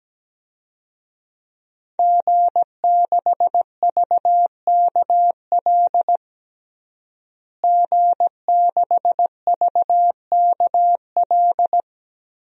Morse code, heard as a keyed tone, about 17 wpm, 700 Hz. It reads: G6VKL G6VKL
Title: DXCC-100 Call Signs - 5 Characters - Repeated 3x - 17wpm